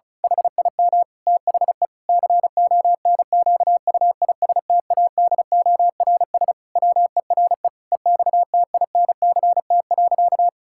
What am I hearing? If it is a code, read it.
Morse code, heard as a keyed tone, about 34 wpm, 700 Hz. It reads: HIM THE CONQUISTADORS WERE EXTINCT.
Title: Pangram Challenge v2 - 35wpm